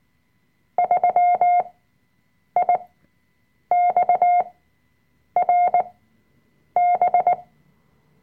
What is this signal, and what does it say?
Morse code, heard as a keyed tone, about 19 wpm, spaced as for 7 wpm, 710 Hz. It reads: 3IXRB
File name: Workshop Dateien / CW-Online-Kurs / Lektion 005